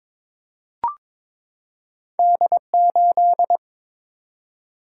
Morse code, heard as a keyed tone, about 22 wpm, 700 Hz. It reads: D8
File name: practice / 22wpm / Two Letter-Number Combinations - 22wpm